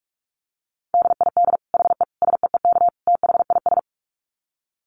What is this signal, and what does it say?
Morse code, heard as a keyed tone, about 44 wpm, 700 Hz. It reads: DID HE SEEK THIS
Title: Dit Challenge 45wpm